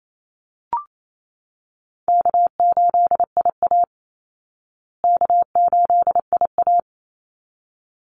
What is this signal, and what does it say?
Morse code, heard as a keyed tone, about 28 wpm, 700 Hz. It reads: K8IA K8IA
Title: Call Signs Repeated 3x - 28wpm